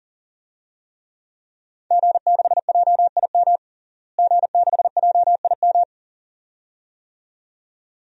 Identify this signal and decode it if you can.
Morse code, heard as a keyed tone, about 40 wpm, 700 Hz. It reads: G6JIM G6JIM
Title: DXCC-100 Call Signs - 5 Characters - Repeated 2x - 40wpm